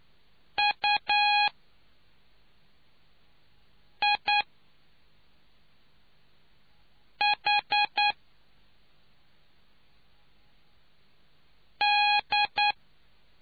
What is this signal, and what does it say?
Morse code, heard as a keyed tone, about 9 wpm, spaced as for 3 wpm, 830 Hz. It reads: UIH D